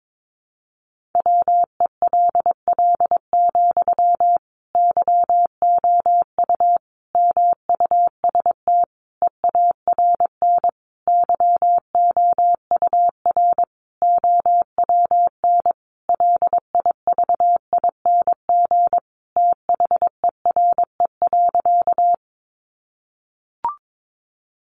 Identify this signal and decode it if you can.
Morse code, heard as a keyed tone, about 22 wpm, 700 Hz. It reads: WELL, YOU MUST EARN YOUR OWN LIVING THERE.
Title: Sentences from Top 1500 Words 22wpm